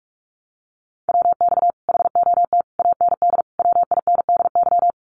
Morse code, heard as a keyed tone, about 44 wpm, 700 Hz. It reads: WX HOT AND WINDY
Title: Common QSO Phrases Encore 45wpm